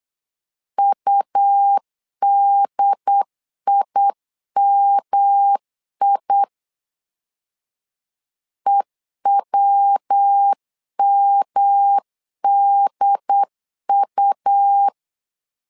Morse code, heard as a keyed tone, about 8 wpm, 800 Hz. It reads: UDIMI EWMDU
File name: LEZIONI / 3 MWRO / 3 MWR 8wpm 2444